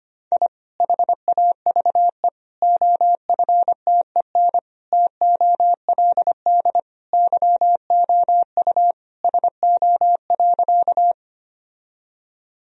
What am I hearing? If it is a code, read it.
Morse code, heard as a keyed tone, about 25 wpm, 700 Hz. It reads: I HAVE OFTEN TOLD YOU SO.
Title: Sentences from Top 1000 Words - Review Part 1 - 25wpm